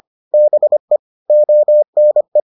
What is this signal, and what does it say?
Morse code, heard as a keyed tone, about 25 wpm, 600 Hz. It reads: BE ONE